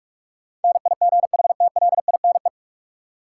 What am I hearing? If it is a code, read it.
Morse code, heard as a keyed tone, about 45 wpm, 700 Hz. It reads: NIGHTLINE
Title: Sets of 5 Words - New Word Drill - 45wpm